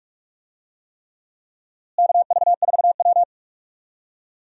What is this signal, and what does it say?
Morse code, heard as a keyed tone, about 46 wpm, 700 Hz. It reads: KV4W
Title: USA Call Signs - 4 Characters - Repeated 3x - 45wpm